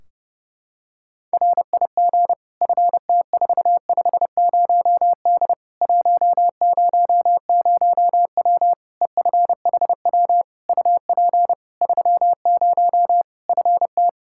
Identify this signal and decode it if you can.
Morse code, heard as a keyed tone, about 30 wpm, 700 Hz. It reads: RIG FT450D 100W EFHW UP 30 FT
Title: K1USN SST QSO Practice 30wpm